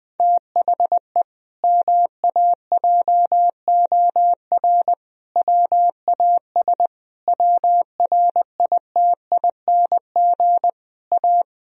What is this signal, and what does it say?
Morse code, heard as a keyed tone, about 20 wpm, 700 Hz. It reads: THE MAJOR WAS WRITING A
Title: Sentences from Top 1000 Words - Review Part 4 - 20wpm